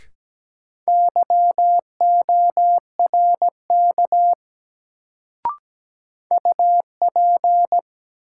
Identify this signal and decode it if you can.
Morse code, heard as a keyed tone, about 17 wpm, 700 Hz. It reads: YORK UP
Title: Sets of 2 Words - New Word Drill - 17wpm